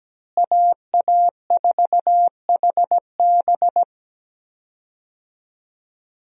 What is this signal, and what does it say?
Morse code, heard as a keyed tone, about 17 wpm, 700 Hz. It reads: AA4HB
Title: USA Call Signs - 5 Characters - Repeated 2x - 17wpm